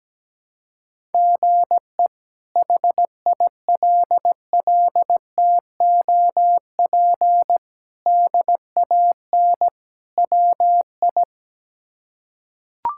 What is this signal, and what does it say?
Morse code, heard as a keyed tone, about 17 wpm, 700 Hz. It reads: GE HILLTOP DAN WI
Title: K1USN SST QSO Practice - 17wpm